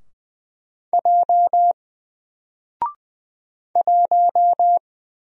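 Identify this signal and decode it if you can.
Morse code, heard as a keyed tone, about 20 wpm, 700 Hz. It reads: J 1
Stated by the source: Single letter and number Morse drill at 20 wpm, from N4LCD.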